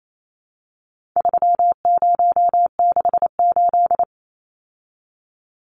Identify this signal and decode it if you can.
Morse code, heard as a keyed tone, about 28 wpm, 700 Hz. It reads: 3068
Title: Numbers - 4-Digit 28wpm